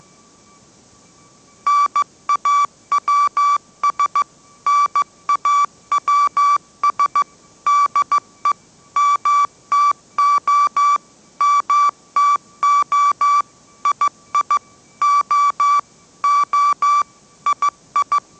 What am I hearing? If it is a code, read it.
Morse code, heard as a keyed tone, about 15 wpm, 1160 Hz. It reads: NAWS NAWS DE MTO MTO II O O II
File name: mrx / mrx066 16wpm